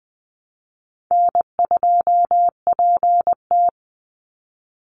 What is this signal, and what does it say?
Morse code, heard as a keyed tone, about 20 wpm, 700 Hz. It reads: N2PT
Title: USA Call Signs 20wpm